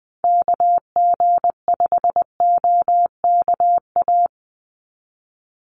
Morse code, heard as a keyed tone, about 20 wpm, 700 Hz. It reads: KG5OKA
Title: USA Call Signs 20wpm